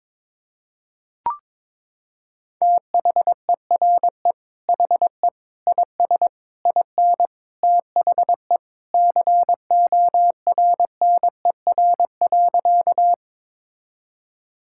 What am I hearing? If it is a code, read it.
Morse code, heard as a keyed tone, about 22 wpm, 700 Hz. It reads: THERE HE IS IN THE CORNER.
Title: Sentences from Top 1600 Words 22wpm